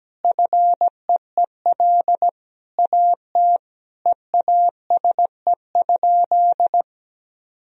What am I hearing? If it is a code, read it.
Morse code, heard as a keyed tone, about 17 wpm, 700 Hz. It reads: FEEL AT EASE?
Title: Sentences from Top 1000 Words - Review Part 1 - 17wpm